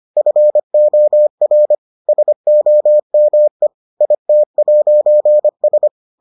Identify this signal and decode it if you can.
Morse code, heard as a keyed tone, about 25 wpm, 600 Hz. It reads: FOR SOME IT'S